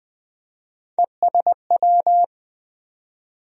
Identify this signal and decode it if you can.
Morse code, heard as a keyed tone, about 20 wpm, 700 Hz. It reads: ESW
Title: USA Call Sign Postfixes - 3 Letters 20wpm